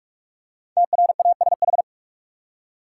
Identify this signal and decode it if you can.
Morse code, heard as a keyed tone, about 46 wpm, 700 Hz. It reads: TRASH
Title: Five Letter Words - 45wpm